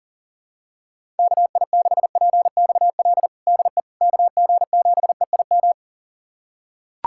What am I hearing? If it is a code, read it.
Morse code, heard as a keyed tone, about 40 wpm, 700 Hz. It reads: KI6PXL DE KG7EIM
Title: USA Callsign DE Callsign 40wpm